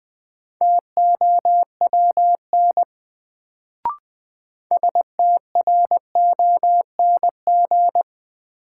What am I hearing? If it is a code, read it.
Morse code, heard as a keyed tone, about 20 wpm, 700 Hz. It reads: TOWN STRONG
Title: Sentences from Top 500 Words - New Word Drill - 20wpm